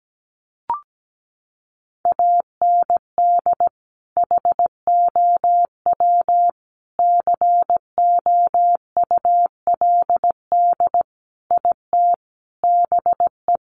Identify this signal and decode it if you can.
Morse code, heard as a keyed tone, about 17 wpm, 700 Hz. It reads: AND HOW COULD IT BE
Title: Sentences from Top 1200 Words 17wpm